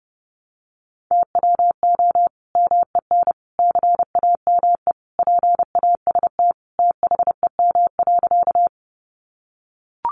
Morse code, heard as a keyed tone, about 30 wpm, 700 Hz. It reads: TWO MEN CAME PAST THEM.
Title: Sentences from Top 600 Words 30wpm